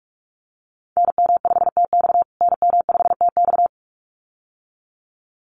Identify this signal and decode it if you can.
Morse code, heard as a keyed tone, about 44 wpm, 700 Hz. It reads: NM5TX NM5TX
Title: USA Call Signs Repeated 2x 45wpm